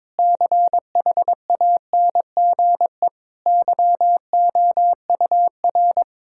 Morse code, heard as a keyed tone, about 22 wpm, 700 Hz. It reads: CHANGE YOUR
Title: Sentences from Top 1000 Words - Review Part 6 - 22wpm